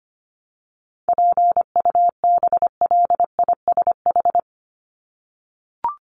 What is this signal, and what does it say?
Morse code, heard as a keyed tone, about 25 wpm, 700 Hz. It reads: PUBLISH
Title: ED Suffix - Top 100 Words with Base Words 25wpm